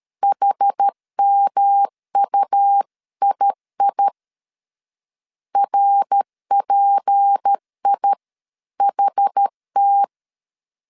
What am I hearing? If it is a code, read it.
Morse code, heard as a keyed tone, about 13 wpm, 800 Hz. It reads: HMUII RPIHT
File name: CW / DET / 1 6 1500C S A 12wpm 3440